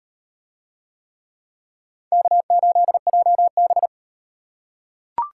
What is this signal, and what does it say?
Morse code, heard as a keyed tone, about 35 wpm, 700 Hz. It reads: K8JB